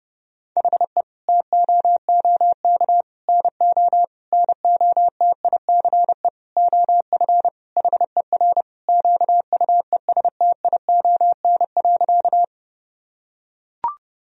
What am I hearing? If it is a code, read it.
Morse code, heard as a keyed tone, about 30 wpm, 700 Hz. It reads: HE TOOK NO NOTICE OF HER QUESTION.
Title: Sentences from Top 1000 Words - Review Part 2 - 30wpm